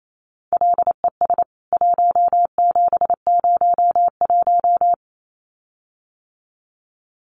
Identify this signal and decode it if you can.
Morse code, heard as a keyed tone, about 28 wpm, 700 Hz. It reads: LES 1701
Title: CWT Exchange 28wpm